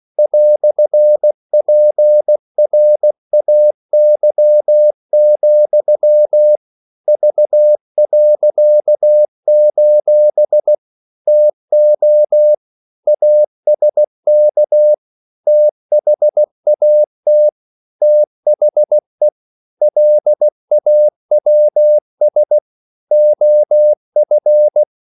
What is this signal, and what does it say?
Morse code, heard as a keyed tone, about 16 wpm, 600 Hz. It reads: "PRAY, V.: TO ASK THAT THE LAWS OF
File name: CW / Practice / EnglishQuotations0416